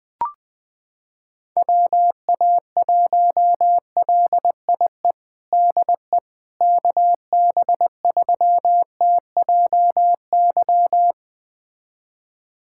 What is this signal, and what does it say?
Morse code, heard as a keyed tone, about 21 wpm, 700 Hz. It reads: WA1LIE DE KB3TJY
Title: USA Callsign DE Callsign - 6 Characters - 20wpm